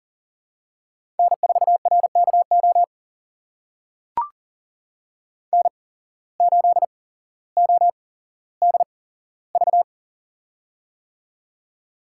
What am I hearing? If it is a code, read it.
Morse code, heard as a keyed tone, about 42 wpm, spaced as for 17 wpm, 700 Hz. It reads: N4RKO N8ODV
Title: USA Call Signs - 5 Characters - 8x Character Spacing - 40wpm